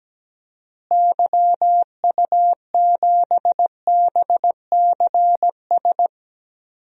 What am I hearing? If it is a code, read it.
Morse code, heard as a keyed tone, about 17 wpm, 700 Hz. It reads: YU7BCS
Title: DXCC-100 Call Signs - 6 Characters - Repeated 2x - 17wpm